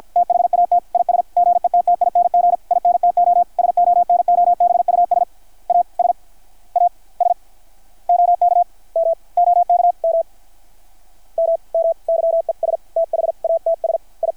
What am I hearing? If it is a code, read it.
Morse code, heard as a keyed tone, about 53 wpm, 660 Hz. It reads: THAT IS GETTING INTO SONOBUS AS AS OK M OK M M M YES THATS I
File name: archived / CRSnet